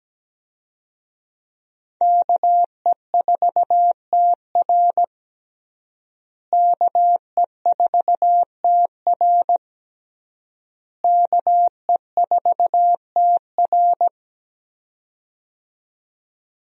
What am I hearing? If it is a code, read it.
Morse code, heard as a keyed tone, about 17 wpm, 700 Hz. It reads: KE4TR KE4TR KE4TR
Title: USA Call Signs - 5 Characters - Repeated 3x - 17wpm